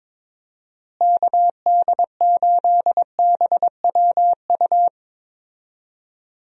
Morse code, heard as a keyed tone, about 22 wpm, 700 Hz. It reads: KD8BWU
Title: USA Call Signs - 6 Characters - Repeated 2x - 22wpm